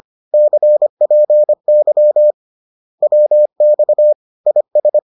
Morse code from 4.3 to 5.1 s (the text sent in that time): IS